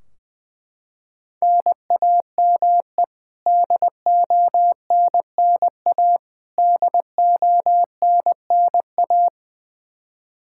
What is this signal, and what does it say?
Morse code, heard as a keyed tone, about 21 wpm, 700 Hz. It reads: NAME DONNA DONNA
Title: K1USN SST QSO Practice 20wpm